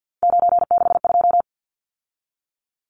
Morse code, heard as a keyed tone, about 50 wpm, 700 Hz. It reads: 962
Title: Numbers - 3-Digit 50wpm